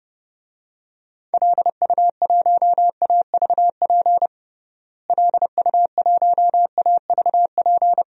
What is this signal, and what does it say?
Morse code, heard as a keyed tone, about 32 wpm, 700 Hz. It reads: LU1AVP LU1AVP
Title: DXCC-100 Call Signs - 6 Characters - Repeated 3x - 30wpm